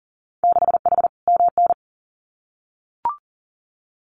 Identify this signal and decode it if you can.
Morse code, heard as a keyed tone, about 42 wpm, 700 Hz. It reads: 6H MN
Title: Winter Field Day Exchanges Encore 40wpm